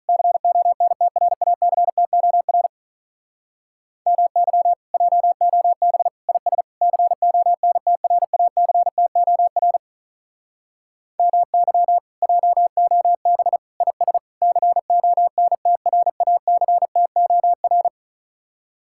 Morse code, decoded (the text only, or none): CONTRACTOR MY JOB IS CONTRACTOR MY JOB IS CONTRACTOR